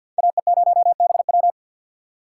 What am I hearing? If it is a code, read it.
Morse code, heard as a keyed tone, about 51 wpm, 690 Hz. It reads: AE0BW
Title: USA Call Signs - 5 Characters - 3x Character Spacing - 50wpm